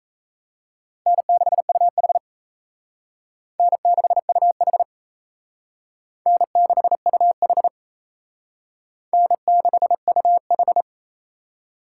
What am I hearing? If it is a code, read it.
Morse code, heard as a keyed tone, about 37 wpm, 700 Hz. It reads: N6UH N6UH N6UH N6UH